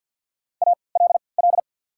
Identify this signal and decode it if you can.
Morse code, heard as a keyed tone, about 48 wpm, spaced as for 28 wpm, 700 Hz. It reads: ALL
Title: 3x Character Spacing - Top 200 Words - 50wpm